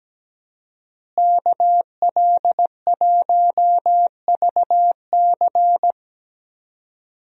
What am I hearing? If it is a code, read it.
Morse code, heard as a keyed tone, about 17 wpm, 700 Hz. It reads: KL1VC
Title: USA Call Signs - 5 Characters - Repeated 2x - 17wpm